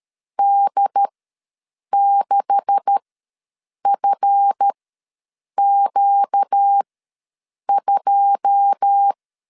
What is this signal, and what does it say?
Morse code, heard as a keyed tone, about 13 wpm, spaced as for 7 wpm, 800 Hz. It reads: D6FQ2
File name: LEZIONI / 9 286 ws 12wpm 3351